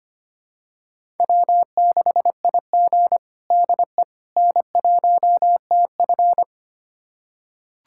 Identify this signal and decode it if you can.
Morse code, heard as a keyed tone, about 25 wpm, 700 Hz. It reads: W6IG DE N1TF
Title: USA Callsign DE Callsign - 4 Characters - 25wpm